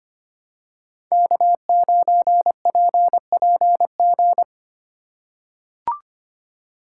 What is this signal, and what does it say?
Morse code, heard as a keyed tone, about 25 wpm, 700 Hz. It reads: K9PPG
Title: USA Call Signs - 3x Character Spacing - 25wpm